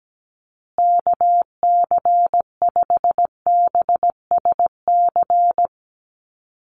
Morse code, heard as a keyed tone, about 17 wpm, 700 Hz. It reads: KC5BSC
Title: USA Call Signs Repeated 3x 17wpm